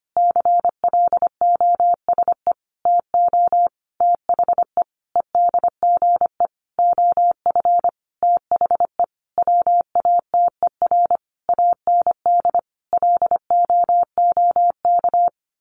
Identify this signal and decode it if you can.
Morse code, heard as a keyed tone, about 25 wpm, 700 Hz. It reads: CLOSE TO THE EDGE OF THE WATER AND LOOK
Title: Sentences from Top 1300 Words 25wpm